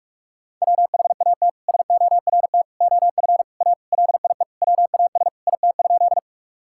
Morse code, heard as a keyed tone, about 42 wpm, 700 Hz. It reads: WHAT SORT OF A LIE WAS IT?
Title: Sentences from Top 1600 Words 45wpm